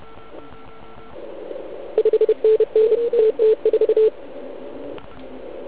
Morse code, heard as a keyed tone, about 33 wpm, 400 Hz. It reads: 5NN TT4
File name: Archiv / CUC149 / OK1IF CUC149